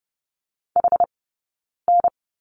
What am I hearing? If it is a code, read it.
Morse code, heard as a keyed tone, about 30 wpm, 700 Hz. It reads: H N
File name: Lesson 20d - Number 3 - Two Characters 30wpm